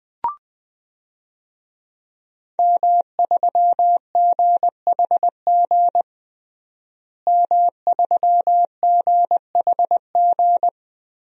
Morse code, heard as a keyed tone, about 20 wpm, 700 Hz. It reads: M3GHG M3GHG
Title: DXCC-100 Call Signs - 5 Characters - Repeated 2x - 20wpm